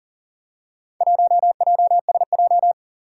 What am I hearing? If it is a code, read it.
Morse code, heard as a keyed tone, about 40 wpm, 700 Hz. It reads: 1JSJ
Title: Four Challenging Letter-Number Combinations 40wpm